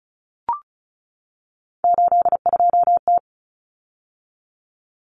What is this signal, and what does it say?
Morse code, heard as a keyed tone, about 34 wpm, 700 Hz. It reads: 82T
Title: Common Cut Numbers - 3-Digit 35wpm